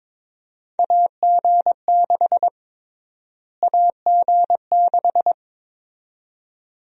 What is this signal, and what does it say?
Morse code, heard as a keyed tone, about 21 wpm, 700 Hz. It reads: AG6 AG6
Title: USA Call Sign Prefixes - Repeated 2x - 22wpm